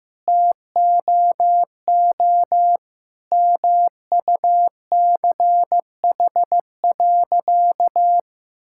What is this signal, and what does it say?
Morse code, heard as a keyed tone, about 15 wpm, 690 Hz. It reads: TOO MUCH.
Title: Sentences from Top 1000 Words - Review Part 5 - 15wpm